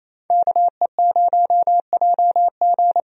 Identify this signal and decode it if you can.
Morse code, heard as a keyed tone, about 28 wpm, 700 Hz. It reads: KE0JG
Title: USA Call Signs - 5 Characters - Repeated 3x - 28wpm